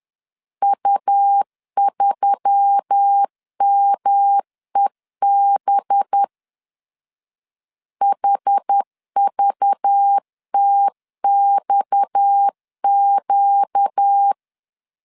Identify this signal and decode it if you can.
Morse code, heard as a keyed tone, about 10 wpm, 800 Hz. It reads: U3MEB HVTXQ
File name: CW / DET / 1 6 1500C S B 10wpm 4137